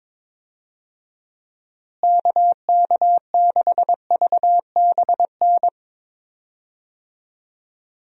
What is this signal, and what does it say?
Morse code, heard as a keyed tone, about 21 wpm, 700 Hz. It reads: KK6VBN
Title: USA Call Signs - 6 Characters - Repeated 3x - 22wpm